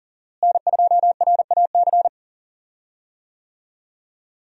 Morse code, heard as a keyed tone, about 37 wpm, 700 Hz. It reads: N2RAC
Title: USA Call Signs - 5 Characters - Repeated 3x - 40wpm